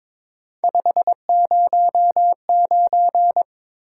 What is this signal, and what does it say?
Morse code, heard as a keyed tone, about 22 wpm, 700 Hz. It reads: 509
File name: practice / 22wpm / Three Letter-Number Combinations - 22wpm